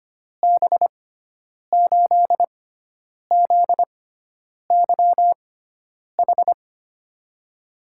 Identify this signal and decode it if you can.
Morse code, heard as a keyed tone, about 25 wpm, spaced as for 9 wpm, 700 Hz. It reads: B8ZYH
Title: USA Call Signs - 6 Characters - 6x Character Spacing - 25wpm